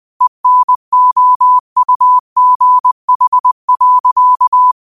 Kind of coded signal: Morse code